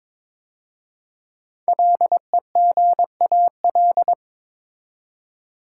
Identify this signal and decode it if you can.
Morse code, heard as a keyed tone, about 22 wpm, 700 Hz. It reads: LEGAL
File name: 5 Letter Words - First 3 Letters Given - 22wpm